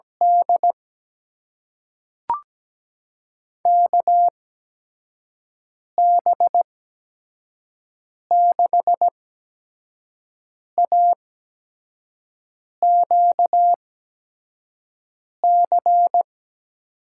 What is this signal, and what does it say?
Morse code, heard as a keyed tone, about 17 wpm, spaced as for 5 wpm, 700 Hz. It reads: D KB6AQC
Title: USA Call Signs - 6 Characters - 8x Character Spacing - 17wpm